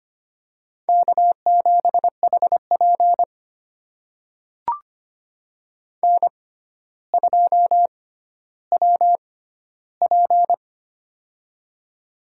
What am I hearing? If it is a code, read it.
Morse code, heard as a keyed tone, about 25 wpm, spaced as for 13 wpm, 700 Hz. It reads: K7HP N2WP
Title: USA Call Signs - 4 Characters - 6x Character Spacing - 25wpm